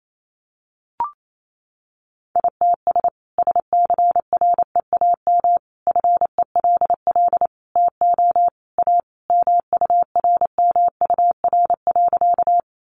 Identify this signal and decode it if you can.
Morse code, heard as a keyed tone, about 28 wpm, 700 Hz. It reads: ITS SCREAM FELL TO A MURMUR.
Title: A Sound Of Thunder 28wpm